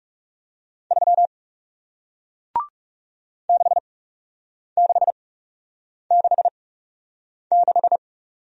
Morse code, heard as a keyed tone, about 40 wpm, spaced as for 8 wpm, 700 Hz. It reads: 3 6666